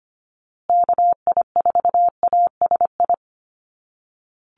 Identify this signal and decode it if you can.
Morse code, heard as a keyed tone, about 25 wpm, 700 Hz. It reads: KI4ASI
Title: USA Call Signs Repeated 3x 25wpm